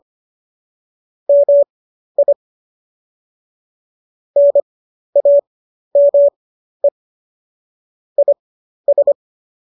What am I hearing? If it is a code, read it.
Morse code, heard as a keyed tone, about 25 wpm, spaced as for 12 wpm, 570 Hz. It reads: MI NAME IS